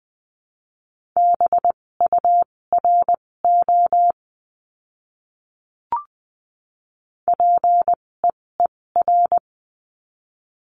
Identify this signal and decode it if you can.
Morse code, heard as a keyed tone, about 20 wpm, spaced as for 16 wpm, 700 Hz. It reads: BURO PEER
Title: 256 Lesson 42a - Words 12-20wpm